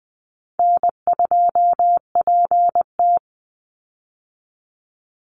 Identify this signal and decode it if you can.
Morse code, heard as a keyed tone, about 19 wpm, 700 Hz. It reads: N2PT
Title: USA Call Signs 20wpm